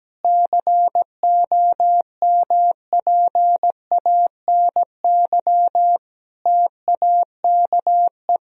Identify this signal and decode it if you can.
Morse code, heard as a keyed tone, about 17 wpm, 700 Hz. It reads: COMPANY TAKE
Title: Sentences from Top 1000 Words - Review Part 4 - 17wpm